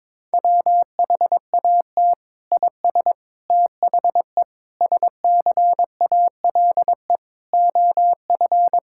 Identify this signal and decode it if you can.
Morse code, heard as a keyed tone, about 22 wpm, 700 Hz. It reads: WHAT IS THE SCALE OF
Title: Sentences from Top 1000 Words - Review Part 5 - 22wpm